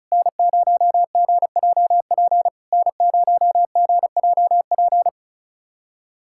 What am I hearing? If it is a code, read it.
Morse code, heard as a keyed tone, about 34 wpm, 690 Hz. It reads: N0GJP N0GJP
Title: SOTA QSO Practice 35wpm